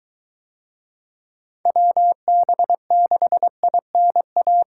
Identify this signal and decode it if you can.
Morse code, heard as a keyed tone, about 21 wpm, 700 Hz. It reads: WB6INA